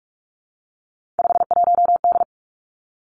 Morse code, heard as a keyed tone, about 46 wpm, 700 Hz. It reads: 51D
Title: Three Challenging Letter-Number Combinations 45wpm